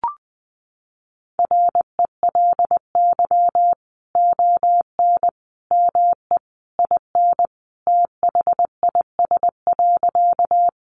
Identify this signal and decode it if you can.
Morse code, heard as a keyed tone, about 20 wpm, 700 Hz. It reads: RELY ON ME IN THIS.
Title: Sentences from Top 1200 Words 20wpm